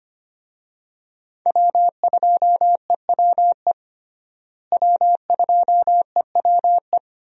Morse code, heard as a keyed tone, about 25 wpm, 700 Hz. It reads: W2EWE W2EWE
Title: USA Call Signs - 5 Characters - Repeated 3x - 25wpm